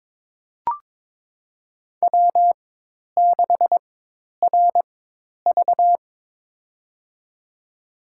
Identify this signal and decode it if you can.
Morse code, heard as a keyed tone, about 21 wpm, spaced as for 10 wpm, 700 Hz. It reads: W6RV
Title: USA Call Signs - 4 Characters - 4x Character Spacing - 22wpm